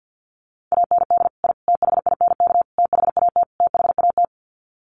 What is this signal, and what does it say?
Morse code, heard as a keyed tone, about 50 wpm, 700 Hz. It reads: AND I THINK THAT THAT
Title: Sets of 5 Words Encore 50wpm